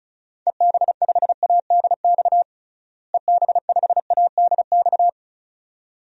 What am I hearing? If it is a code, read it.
Morse code, heard as a keyed tone, about 36 wpm, 700 Hz. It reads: EB5ADX EB5ADX
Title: DXCC-100 Call Signs - 6 Characters - Repeated 2x - 35wpm